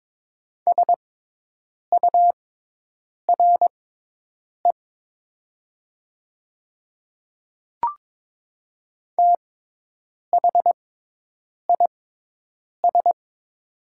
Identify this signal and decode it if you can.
Morse code, heard as a keyed tone, about 22 wpm, spaced as for 8 wpm, 700 Hz. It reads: SURE THIS
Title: 6x Character Spacing - Top 300 Words - 22wpm